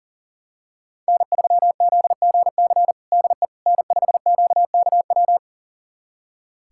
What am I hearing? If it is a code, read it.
Morse code, heard as a keyed tone, about 40 wpm, 700 Hz. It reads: N3ZGC DE N5QKW
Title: USA Callsign DE Callsign - 5 Characters - 40wpm